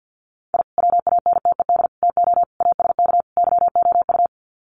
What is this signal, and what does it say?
Morse code, heard as a keyed tone, about 49 wpm, 700 Hz. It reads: I WANTED TO ASK YOU
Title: Sets of 5 Words Encore2 50wpm